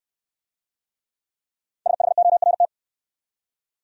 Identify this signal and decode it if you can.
Morse code, heard as a keyed tone, about 63 wpm, 700 Hz. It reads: SHORT